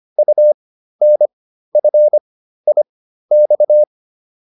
Morse code, heard as a keyed tone, about 25 wpm, spaced as for 13 wpm, 600 Hz. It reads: UNFIX
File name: UN5 13